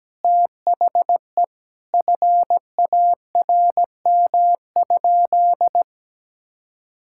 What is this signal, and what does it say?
Morse code, heard as a keyed tone, about 17 wpm, 700 Hz. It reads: THE FARM?
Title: Sentences from Top 1000 Words - Review Part 5 - 17wpm